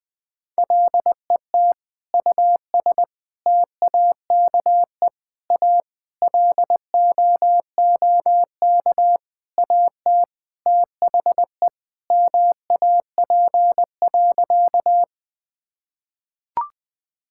Morse code, heard as a keyed tone, about 19 wpm, 700 Hz. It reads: LET US TAKE A LOOK AT THE MAP.
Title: Sentences with 5-Word Phrases - 20wpm